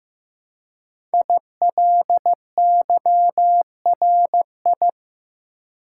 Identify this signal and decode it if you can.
Morse code, heard as a keyed tone, about 15 wpm, 700 Hz. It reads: ILYRI
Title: Code Groups - 5 Letters - 15wpm